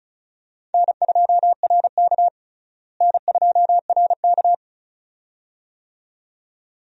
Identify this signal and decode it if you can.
Morse code, heard as a keyed tone, about 36 wpm, 700 Hz. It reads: N2RK N2RK
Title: USA Call Signs - 4 Characters - Repeated 2x - 35wpm